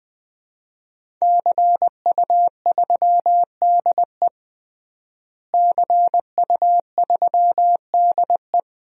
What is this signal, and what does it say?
Morse code, heard as a keyed tone, about 20 wpm, 700 Hz. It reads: CU3DE CU3DE
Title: DXCC-100 Call Signs - 5 Characters - Repeated 3x - 20wpm